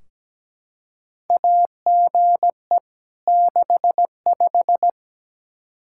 Morse code, heard as a keyed tone, about 17 wpm, 700 Hz. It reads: AGE 65
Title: K1USN SST QSO Practice 17wpm